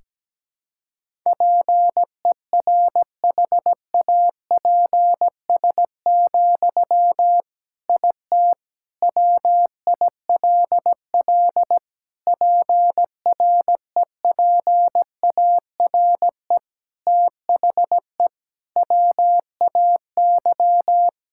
Morse code, heard as a keyed tone, about 17 wpm, 700 Hz. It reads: PERHAPS, IT WILL PREPARE THE WAY
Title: Sentences from Top 1000 Words - Review Part 6 - 17wpm